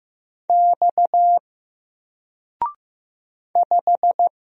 Morse code, heard as a keyed tone, about 15 wpm, 700 Hz. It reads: X 5
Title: Single Character (Letter:Number:Punctuation:Prosign) 15wpm